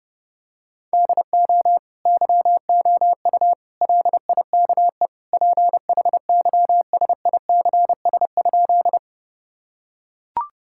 Morse code, heard as a keyed tone, about 30 wpm, 700 Hz. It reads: DO YOU LIKE PHYSICS?
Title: Sentences from Top 1000 Words - Review Part 5 - 30wpm